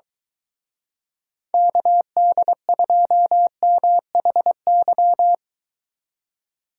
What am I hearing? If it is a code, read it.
Morse code, heard as a keyed tone, about 22 wpm, 700 Hz. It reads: KD2MHY